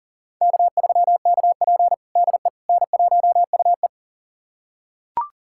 Morse code, heard as a keyed tone, about 40 wpm, 700 Hz. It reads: K3KP DE N1UE
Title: USA Callsign DE Callsign - 4 Characters - 40wpm